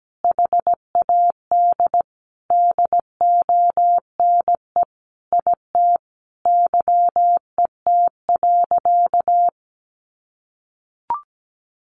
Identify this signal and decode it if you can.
Morse code, heard as a keyed tone, about 17 wpm, 700 Hz. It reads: HAD DONE IT YET.
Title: Sentences from Top 1100 Words 17wpm